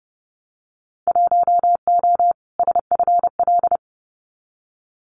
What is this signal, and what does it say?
Morse code, heard as a keyed tone, about 30 wpm, 700 Hz. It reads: 1O SFL
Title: Winter Field Day Exchanges 30wpm